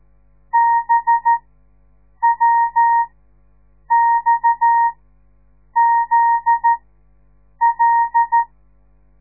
Morse code, heard as a keyed tone, about 13 wpm, spaced as for 7 wpm, 930 Hz. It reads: BWXZL
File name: radiomorningstar / morse / les 9